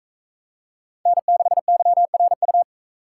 Morse code, heard as a keyed tone, about 38 wpm, 700 Hz. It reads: N6YRU